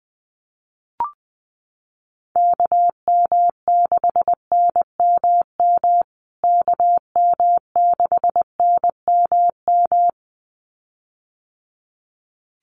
Morse code, heard as a keyed tone, about 20 wpm, 700 Hz. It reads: KM6NMM KM6NMM
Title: USA Call Signs Repeated 2x 20wpm